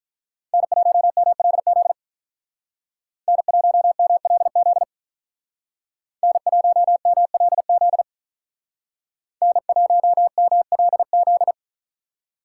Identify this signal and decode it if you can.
Morse code, heard as a keyed tone, about 43 wpm, 700 Hz. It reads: N1MLZ N1MLZ N1MLZ N1MLZ
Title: Speed Racing - USA Call Signs 35wpm